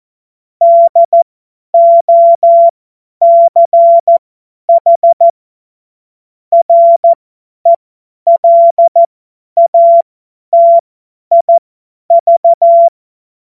Morse code, heard as a keyed tone, about 14 wpm, spaced as for 10 wpm, 680 Hz. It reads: DOCH RELATIV
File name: Morsekurs / Witze / Witz 12. April 2022 50 F70